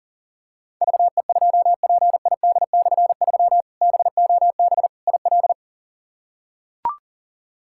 Morse code, heard as a keyed tone, about 40 wpm, 700 Hz. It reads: VE2PID/3 BOB IL
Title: K1USN SST QSO Practice - 40wpm